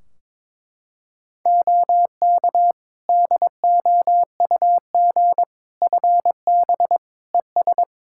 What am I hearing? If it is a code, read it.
Morse code, heard as a keyed tone, about 21 wpm, 700 Hz. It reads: OK DOUG FB ES
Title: Practice QSO - Exchange 2 - Info - 22wpm